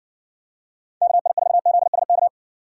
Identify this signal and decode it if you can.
Morse code, heard as a keyed tone, about 60 wpm, 700 Hz. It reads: KI4ZSK